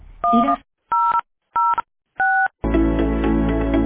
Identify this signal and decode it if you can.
DTMF, heard as telephone touch tones, tones of about 300 ms. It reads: 1006